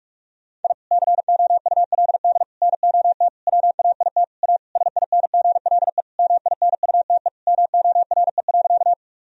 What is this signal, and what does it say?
Morse code, heard as a keyed tone, about 41 wpm, 700 Hz. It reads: I COULD NOT WAIT A SINGLE MINUTE MORE.
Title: Sentences from Top 1000 Words - Review Part 4 - 45wpm